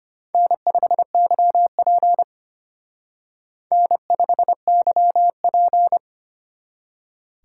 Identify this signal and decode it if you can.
Morse code, heard as a keyed tone, about 32 wpm, 700 Hz. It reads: N5YP N5YP